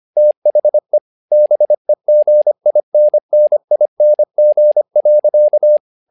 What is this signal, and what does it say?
Morse code, heard as a keyed tone, about 25 wpm, 600 Hz. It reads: THE BEGINNING.
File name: learning center / tower trouble / tower trouble 25wpm